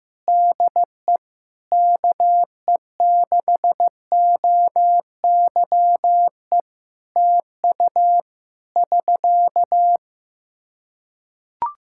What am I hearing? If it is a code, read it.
Morse code, heard as a keyed tone, about 15 wpm, 700 Hz. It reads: DE KE6OYE TU <SK>
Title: Practice QSO - Exchange 4 - Ending - 15wpm